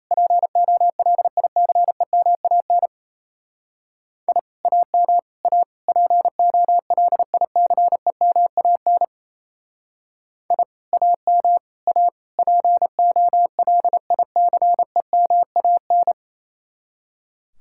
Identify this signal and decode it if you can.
Morse code, heard as a keyed tone, about 33 wpm, 700 Hz. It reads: POLICEMAN I AM A POLICEMAN I AM A POLICEMAN
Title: Speed-Racing Common QSO Phrases Encore - 28wpm